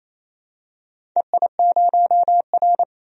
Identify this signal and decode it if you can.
Morse code, heard as a keyed tone, about 28 wpm, 700 Hz. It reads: EI0R
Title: DXCC-100 Call Signs - 4 Characters - 28wpm